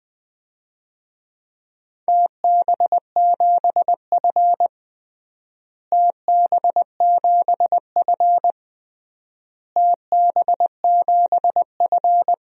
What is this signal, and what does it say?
Morse code, heard as a keyed tone, about 21 wpm, 700 Hz. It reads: TB7F TB7F TB7F
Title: DXCC-100 Call Signs - 4 Characters - Repeated 3x - 20wpm